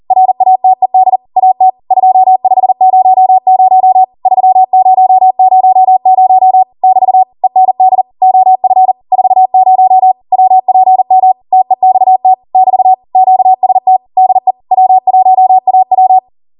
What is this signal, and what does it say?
Morse code, heard as a keyed tone, about 40 wpm, 750 Hz. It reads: RATED AT 2500 3000 <BT> END OF 40 WPM TEXT <BT> QST DE W1AW
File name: Morse / 40WPM